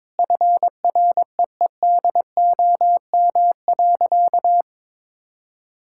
Morse code, heard as a keyed tone, about 22 wpm, 700 Hz. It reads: FREEDOM.
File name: practice / 22wpm / Sentences with 3-Word Phrases - 22wpm